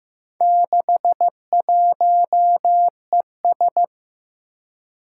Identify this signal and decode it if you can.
Morse code, heard as a keyed tone, about 15 wpm, 700 Hz. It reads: 61ES
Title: Four Challenging Letter-Number Combinations 15wpm